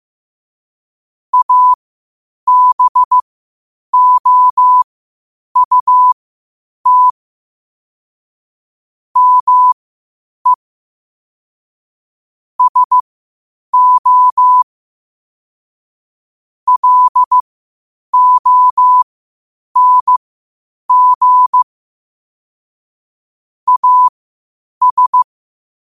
Morse code, heard as a keyed tone, about 15 wpm, spaced as for 8 wpm, 1000 Hz. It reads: ABOUT ME SO LONG AS
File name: qotdfiles / 8wpm